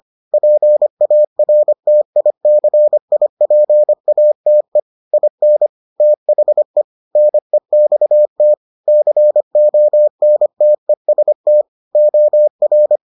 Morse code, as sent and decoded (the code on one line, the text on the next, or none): .--. .- .-. - .. -.-. .. .--. .- - . / .. -. / - .... / -. -..- - / -.-. --- -. - . ... - / --- .-.
PARTICIPATE IN THE NEXT CONTEST OR